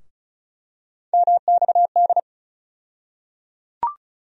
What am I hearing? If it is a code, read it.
Morse code, heard as a keyed tone, about 36 wpm, 700 Hz. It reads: MXD
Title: USA Call Sign Postfixes - 3 Letters 35wpm